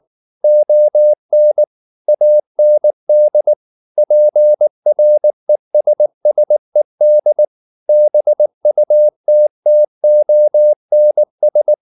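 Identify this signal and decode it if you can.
Morse code, heard as a keyed tone, about 19 wpm, 600 Hz. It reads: ON AND PRESSED BUTTONS
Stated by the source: Morse code at 19 wpm from CW med Gnister, reading Ken Follett's novel Never in English.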